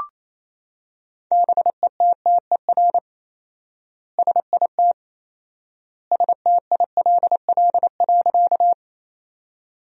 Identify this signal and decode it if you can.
Morse code, heard as a keyed tone, about 27 wpm, 700 Hz. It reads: BETTER SIT STILL.
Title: 4x Word-Spacing - Sentences from Top 500 Words - 28wpm